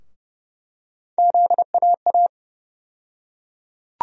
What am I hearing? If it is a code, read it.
Morse code, heard as a keyed tone, about 30 wpm, 700 Hz. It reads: ZAA
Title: USA Call Sign Postfixes - 3 Letters 30wpm